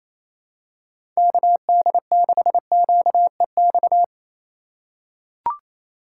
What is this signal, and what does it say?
Morse code, heard as a keyed tone, about 28 wpm, 700 Hz. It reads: KD6QEX
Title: USA Call Signs - 3x Character Spacing - 28wpm